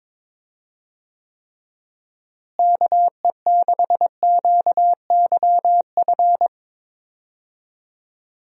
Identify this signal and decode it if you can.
Morse code, heard as a keyed tone, about 22 wpm, 700 Hz. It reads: KE6QYF
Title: USA Call Signs - 6 Characters - Repeated 3x - 22wpm